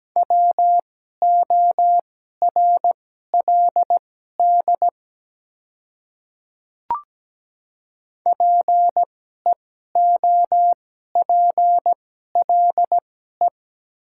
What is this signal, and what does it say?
Morse code, heard as a keyed tone, about 17 wpm, spaced as for 12 wpm, 700 Hz. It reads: WORLD PEOPLE
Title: 2x Character Spacing - Top 200 Words - 17wpm